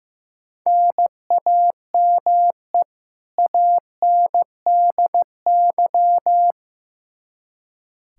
Morse code, heard as a keyed tone, about 15 wpm, 700 Hz. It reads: NAME ANDY
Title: Practice QSO - Exchange 1 - Beginning - 15wpm